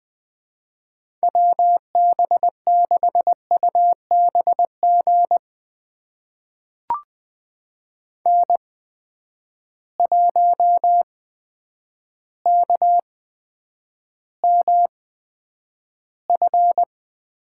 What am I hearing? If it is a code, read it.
Morse code, heard as a keyed tone, about 20 wpm, spaced as for 9 wpm, 700 Hz. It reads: WB6UBG N1KMF
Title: USA Call Signs - 8x Character Spacing - 20wpm